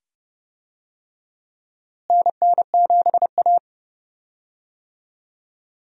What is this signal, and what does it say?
Morse code, heard as a keyed tone, about 30 wpm, 700 Hz. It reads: NN7A
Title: USA Call Signs - 4 Characters - Repeated 2x - 30wpm